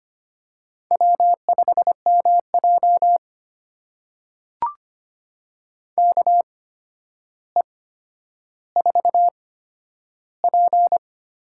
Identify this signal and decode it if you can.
Morse code, heard as a keyed tone, about 25 wpm, spaced as for 11 wpm, 700 Hz. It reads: W5MJ KE4P
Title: USA Call Signs - 4 Characters - 8x Character Spacing - 25wpm